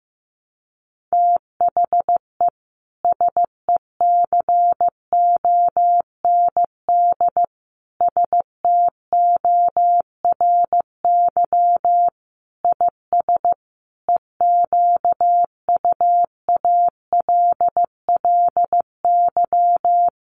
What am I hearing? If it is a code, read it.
Morse code, heard as a keyed tone, about 15 wpm, 700 Hz. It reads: THE SECOND STORY IS EQUALLY
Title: Sentences from Top 1400 Words 15wpm